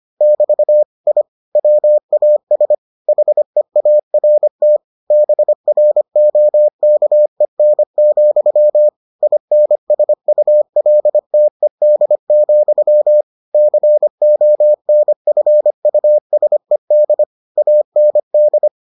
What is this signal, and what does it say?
Morse code, heard as a keyed tone, about 25 wpm, 600 Hz. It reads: <BT> I WAS HEART BROKEN, INSULTED, CONFUSED AND